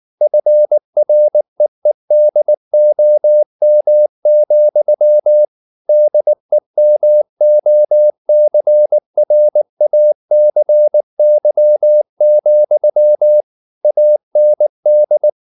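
Morse code, heard as a keyed tone, about 19 wpm, 600 Hz. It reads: FREEDOM, DEMOCRACY, AND
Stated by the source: Morse code at 19 wpm from CW med Gnister, reading Ken Follett's novel Never in English.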